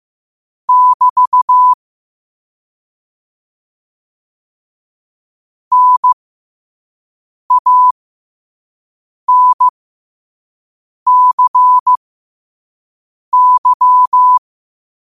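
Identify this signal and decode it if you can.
Morse code, heard as a keyed tone, about 15 wpm, spaced as for 5 wpm, 1000 Hz. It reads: <BT> NANCY
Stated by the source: Morse quote of the day at 5 wpm.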